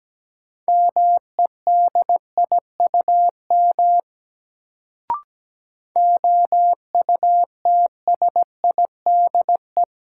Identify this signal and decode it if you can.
Morse code, heard as a keyed tone, about 17 wpm, 700 Hz. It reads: MEDIUM OUTSIDE
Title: Sentences from Top 900 Words - New Word Drill - 17wpm